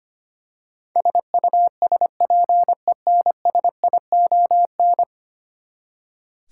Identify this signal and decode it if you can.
Morse code, heard as a keyed tone, about 25 wpm, 700 Hz. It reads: SUSPENSION
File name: practice / 25wpm / Ten Letter Words - 25wpm